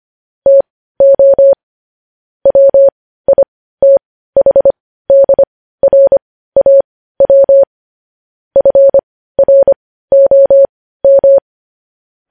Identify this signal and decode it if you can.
Morse code, heard as a keyed tone, about 25 wpm, spaced as for 15 wpm, 550 Hz. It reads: TO WITHDRAW FROM
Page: news headlines in morse code at 15 wpm